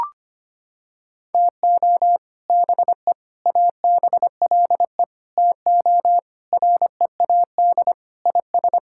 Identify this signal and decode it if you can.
Morse code, heard as a keyed tone, about 25 wpm, 700 Hz. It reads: TO BE ABLE TO READ IS
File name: practice / 25wpm / Sentences with 3-Word Phrases - 25wpm